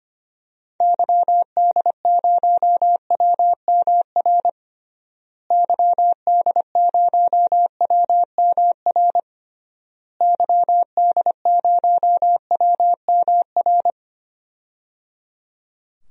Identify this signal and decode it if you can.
Morse code, heard as a keyed tone, about 25 wpm, 700 Hz. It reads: YD0WMR YD0WMR YD0WMR
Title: DXCC-100 Call Signs - 6 Characters - Repeated 3x - 25wpm